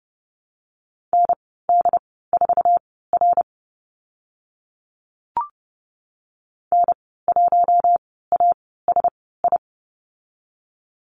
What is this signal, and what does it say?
Morse code, heard as a keyed tone, about 30 wpm, spaced as for 17 wpm, 700 Hz. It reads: ND4R N1ASI
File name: Lesson 15g - Number 5 - Call Signs 30wpm